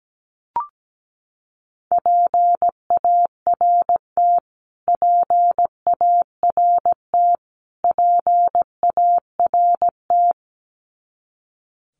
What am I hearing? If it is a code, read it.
Morse code, heard as a keyed tone, about 17 wpm, 700 Hz. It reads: PART PART PART
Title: Top 200 Words Repeated 17wpm